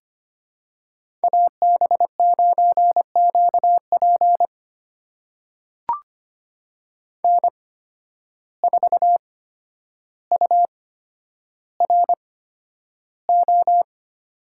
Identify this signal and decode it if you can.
Morse code, heard as a keyed tone, about 25 wpm, spaced as for 11 wpm, 700 Hz. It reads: AB9QP N4URO
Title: USA Call Signs - 5 Characters - 8x Character Spacing - 25wpm